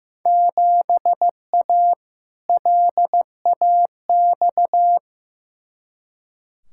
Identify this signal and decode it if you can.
Morse code, heard as a keyed tone, about 16 wpm, 700 Hz. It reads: 7A LAX
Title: Field Day Exchanges - 15wpm